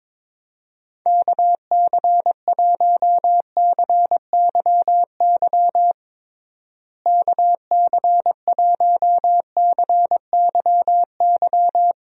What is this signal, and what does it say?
Morse code, heard as a keyed tone, about 21 wpm, 700 Hz. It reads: KC1CYY KC1CYY
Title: USA Call Signs - 6 Characters - Repeated 2x - 22wpm